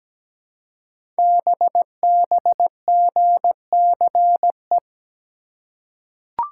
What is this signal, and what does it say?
Morse code, heard as a keyed tone, about 17 wpm, 700 Hz. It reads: BBGCE
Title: Code Groups - 5 Letter:Numbers - 17wpm